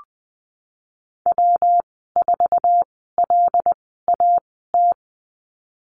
Morse code, heard as a keyed tone, about 20 wpm, spaced as for 15 wpm, 700 Hz. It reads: W4LAT
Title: 140 Lesson 23g - Letter f - Call Signs 20wpm